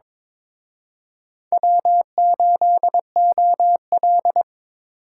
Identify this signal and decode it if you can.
Morse code, heard as a keyed tone, about 26 wpm, 700 Hz. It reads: W8OL
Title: Speed Racing - USA Call Signs 22wpm